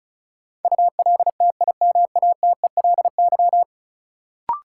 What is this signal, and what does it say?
Morse code, heard as a keyed tone, about 36 wpm, 700 Hz. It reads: ULTIMATELY
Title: Sentences from Top 1200 Words - New Word Drill Alt - 35wpm